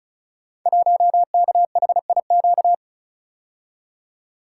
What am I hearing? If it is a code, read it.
Morse code, heard as a keyed tone, about 36 wpm, 700 Hz. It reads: 1KHIQ
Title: Code Groups - 5 Letter:Numbers - 35wpm